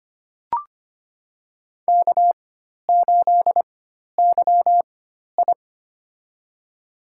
Morse code, heard as a keyed tone, about 25 wpm, spaced as for 12 wpm, 700 Hz. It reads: K8YI
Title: USA Call Signs - 4x Character Spacing - 25wpm